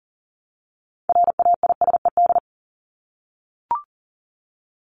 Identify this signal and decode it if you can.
Morse code, heard as a keyed tone, about 40 wpm, 700 Hz. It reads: RAISED
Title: ED Suffix - Top 100 Words with Base Words 40wpm